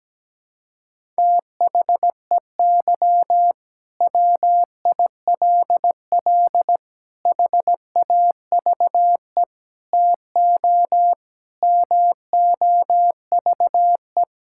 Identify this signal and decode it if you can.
Morse code, heard as a keyed tone, about 17 wpm, 700 Hz. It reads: THEY WILL HAVE TO MOVE
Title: Sentences from Top 1000 Words - Review Part 6 - 17wpm